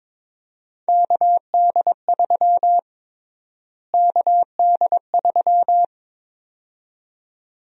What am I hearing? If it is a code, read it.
Morse code, heard as a keyed tone, about 22 wpm, 700 Hz. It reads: KD3 KD3
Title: USA Call Sign Prefixes - Repeated 2x - 22wpm